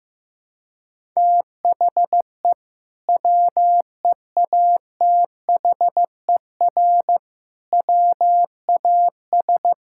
Morse code, heard as a keyed tone, about 15 wpm, 700 Hz. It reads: THE WEATHER WAS
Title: Sentences from Top 1000 Words - Review Part 5 - 15wpm